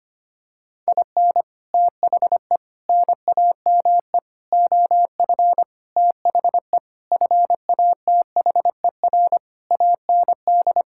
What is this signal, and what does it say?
Morse code, heard as a keyed tone, about 25 wpm, 700 Hz. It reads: IN THE NAME OF THE FATHER AND
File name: practice / 25wpm / Sentences with 9-Word Phrases - 25wpm